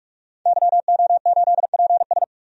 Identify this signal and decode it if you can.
Morse code, heard as a keyed tone, about 46 wpm, 700 Hz. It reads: YO8PS
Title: DXCC-100 Call Signs - 5 Characters - 45wpm